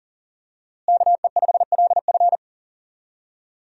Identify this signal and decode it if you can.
Morse code, heard as a keyed tone, about 37 wpm, 700 Hz. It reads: KE5LF
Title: USA Call Signs - 5 Characters - Repeated 2x - 40wpm